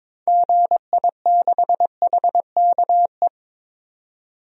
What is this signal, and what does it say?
Morse code, heard as a keyed tone, about 21 wpm, 700 Hz. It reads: GI6HKE
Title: DXCC-100 Call Signs - 6 Characters - 22wpm